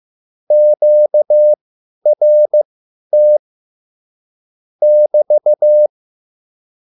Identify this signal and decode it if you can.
Morse code, heard as a keyed tone, about 15 wpm, spaced as for 10 wpm, 600 Hz. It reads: QRT <BT>
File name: QSL / CW lesson